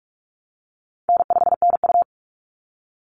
Morse code, heard as a keyed tone, about 44 wpm, 700 Hz. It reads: N5NU
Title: Call Signs Repeated 3x - 45wpm